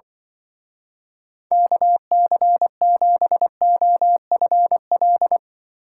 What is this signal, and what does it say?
Morse code, heard as a keyed tone, about 24 wpm, 700 Hz. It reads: KC7OFL
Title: Speed Racing - USA 2x3 Call Signs 20wpm